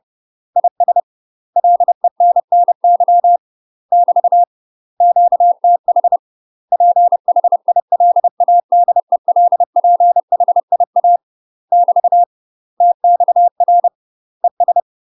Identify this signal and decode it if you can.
Morse code, heard as a keyed tone, about 30 wpm, 700 Hz. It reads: IS LENNY <BT> QTH PHILADELPHIA <BT> TXR ES
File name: UK MASTER QSOs TRAINING / 30WPM / UK MASTER QSOs 0041